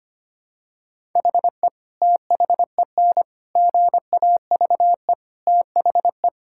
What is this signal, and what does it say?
Morse code, heard as a keyed tone, about 25 wpm, 700 Hz. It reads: HE THEN GAVE THE
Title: Sentences from Top 1000 Words - Review Part 6 - 25wpm